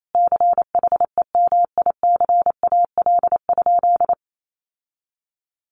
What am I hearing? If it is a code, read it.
Morse code, heard as a keyed tone, about 28 wpm, 700 Hz. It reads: CHEMICAL?
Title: Sentences from Top 1200 Words 28wpm